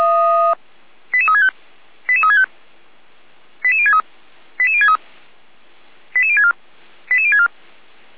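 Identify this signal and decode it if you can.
Five-tone selective call (ZVEI), heard as quick runs of five tones, tones of about 70 ms. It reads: 88365 88365 88063 88063 88064 88064
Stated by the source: ZVEI five-tone call received on an SDR, from the Feuerwehr Orlamünde fire brigade.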